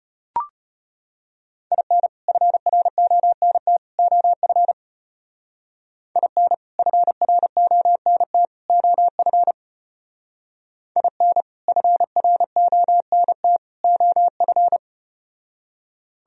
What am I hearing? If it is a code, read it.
Morse code, heard as a keyed tone, about 33 wpm, 700 Hz. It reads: IN FRONT OF IN FRONT OF IN FRONT OF